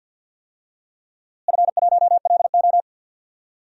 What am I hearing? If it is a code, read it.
Morse code, heard as a keyed tone, about 48 wpm, 700 Hz. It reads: F1LO
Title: DXCC-100 Call Signs - 4 Characters - Repeated 2x - 50wpm